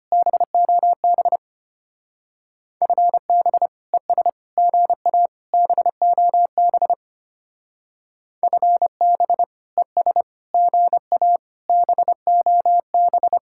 Morse code, heard as a keyed tone, about 31 wpm, 700 Hz. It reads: BOB FB ES GA BOB FB ES GA BOB
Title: Speed-Racing Common QSO Phrases - 25wpm